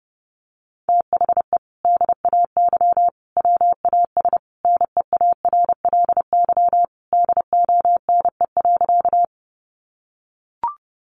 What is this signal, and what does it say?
Morse code, heard as a keyed tone, about 30 wpm, 700 Hz. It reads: THE DAY WAS NEARLY DONE.
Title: Sentences from Top 1300 Words 30wpm